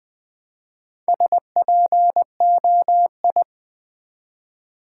Morse code, heard as a keyed tone, about 20 wpm, 700 Hz. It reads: SPOI
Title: Build A Word - 1 Letter At A Time - 5-Letter Words - 20wpm